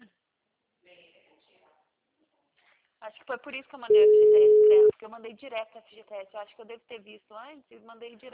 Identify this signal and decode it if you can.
Ringback tone (Europe, 425 Hz), with speech.